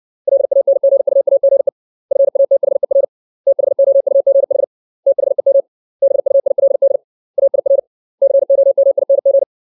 Morse code, heard as a keyed tone, about 60 wpm, 550 Hz. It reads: LANGUAGE FATHER THOUGH THEM BUILD AIR COMING
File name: 632 words / 632 Words 60WPM